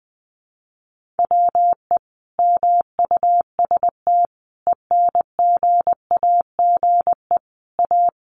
Morse code, heard as a keyed tone, about 20 wpm, 700 Hz. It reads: WE MUST ENGAGE A
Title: Sentences from Top 1500 Words 20wpm